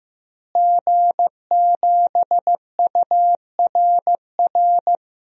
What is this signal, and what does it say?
Morse code, heard as a keyed tone, about 15 wpm, 700 Hz. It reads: G7URR
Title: DXCC-100 Call Signs - 5 Characters - Repeated 3x - 15wpm